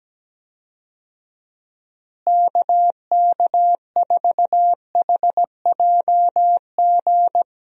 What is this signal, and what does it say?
Morse code, heard as a keyed tone, about 17 wpm, 700 Hz. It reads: KK4HJG
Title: USA Call Signs - 6 Characters - Repeated 3x - 17wpm